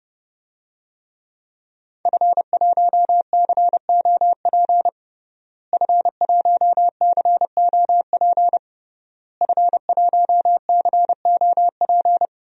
Morse code, heard as a keyed tone, about 32 wpm, 700 Hz. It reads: F1COP F1COP F1COP
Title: DXCC-100 Call Signs - 5 Characters - Repeated 3x - 30wpm